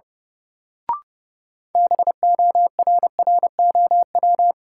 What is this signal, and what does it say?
Morse code, heard as a keyed tone, about 30 wpm, 700 Hz. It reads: BORROW
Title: Sentences from Top 1500 Words - New Word Drill Alt - 30wpm